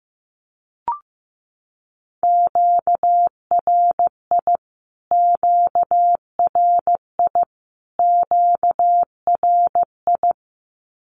Morse code, heard as a keyed tone, about 15 wpm, 700 Hz. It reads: QRI QRI QRI
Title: Q-Codes Repeated 3x 15wpm